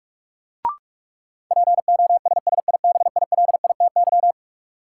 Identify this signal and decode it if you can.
Morse code, heard as a keyed tone, about 45 wpm, 700 Hz. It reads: POSSIBILITY
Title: Sets of 3 Words - New Word Drill - 45wpm